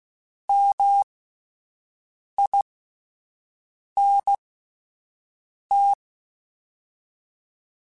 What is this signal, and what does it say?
Morse code, heard as a keyed tone, about 16 wpm, spaced as for 5 wpm, 800 Hz. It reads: MINT